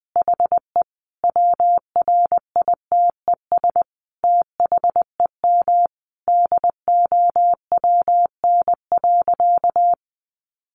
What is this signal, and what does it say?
Morse code, heard as a keyed tone, about 19 wpm, 700 Hz. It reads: HE WRITES THEM DOWN.
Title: Additional Easy Phrases 20wpm